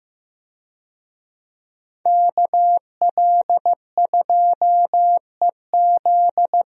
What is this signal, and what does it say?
Morse code, heard as a keyed tone, about 15 wpm, 700 Hz. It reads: KL2EZ
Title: USA Call Signs - 5 Characters - Repeated 3x - 15wpm